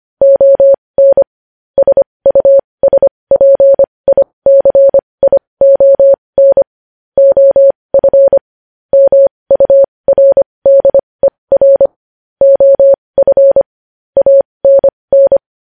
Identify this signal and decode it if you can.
Morse code, heard as a keyed tone, about 25 wpm, spaced as for 20 wpm, 550 Hz. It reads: ON SUSPICION OF MURDER OF ANN
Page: news headlines in morse code at 20 wpm